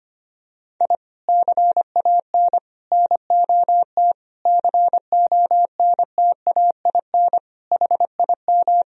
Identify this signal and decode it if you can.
Morse code, heard as a keyed tone, about 25 wpm, 700 Hz. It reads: I CAN NOT CONTAIN HIM
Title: Sentences from Top 1000 Words - Review Part 3 - 25wpm